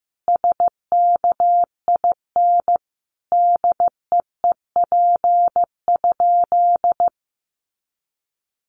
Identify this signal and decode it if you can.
Morse code, heard as a keyed tone, about 15 wpm, 700 Hz. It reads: SKIN DEEP?
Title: Sentences from Top 1300 Words 15wpm